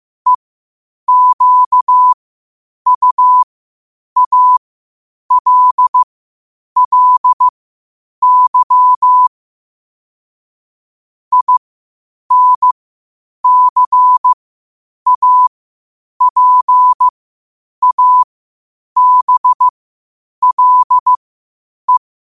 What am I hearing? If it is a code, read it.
Morse code, heard as a keyed tone, about 15 wpm, spaced as for 8 wpm, 1000 Hz. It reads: EQUALLY INCAPABLE
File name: qotdfiles / 8wpm